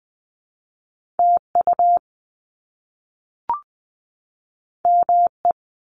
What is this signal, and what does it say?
Morse code, heard as a keyed tone, about 20 wpm, 700 Hz. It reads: TU ME
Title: Top Two Letter Combinations in Words 20wpm